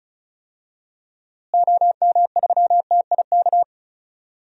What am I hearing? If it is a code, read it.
Morse code, heard as a keyed tone, about 34 wpm, 700 Hz. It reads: OM3TIK
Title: DXCC-100 Call Signs - 6 Characters - Repeated 3x - 35wpm